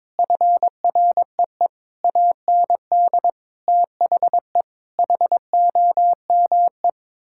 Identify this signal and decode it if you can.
Morse code, heard as a keyed tone, about 22 wpm, 700 Hz. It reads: FREE AND THE HOME
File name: practice / 22wpm / Sets of 8 Words - 22wpm